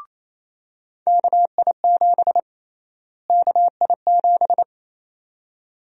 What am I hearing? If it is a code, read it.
Morse code, heard as a keyed tone, about 28 wpm, 700 Hz. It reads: KI7 KI7
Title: USA Call Sign Prefixes - Repeated 2x - 28wpm